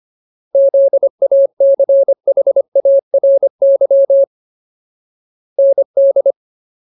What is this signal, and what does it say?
Morse code, heard as a keyed tone, about 25 wpm, 550 Hz. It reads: ZACHARY ND
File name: Ultimate Word List / 25WPM Ultimate